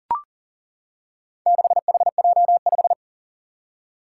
Morse code, heard as a keyed tone, about 40 wpm, 700 Hz. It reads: 6HJ5
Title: Four Challenging Letter-Number Combinations 40wpm